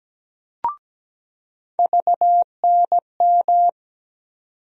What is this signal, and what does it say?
Morse code, heard as a keyed tone, about 17 wpm, 700 Hz. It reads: VNM
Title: USA Call Sign Postfixes - 3 Letters 17wpm